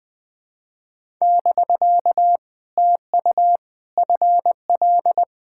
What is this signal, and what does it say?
Morse code, heard as a keyed tone, about 20 wpm, 700 Hz. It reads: <BK> TU FL
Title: POTA QSO Practice - 20wpm